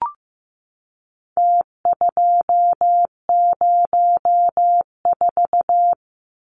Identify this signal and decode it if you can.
Morse code, heard as a keyed tone, about 15 wpm, 700 Hz. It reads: T204
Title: Numbers - Contest Style - 4-Digit 15wpm